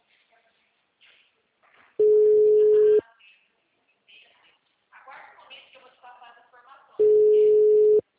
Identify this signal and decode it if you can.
Ringback tone (Europe, 425 Hz), with speech.
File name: out- -2027- - - .90086